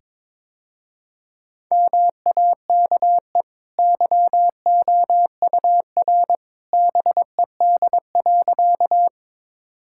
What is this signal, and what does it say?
Morse code, heard as a keyed tone, about 22 wpm, 700 Hz. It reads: MAKE YOUR BED.
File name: practice / 22wpm / Easy Sentences with Hints - 22wpm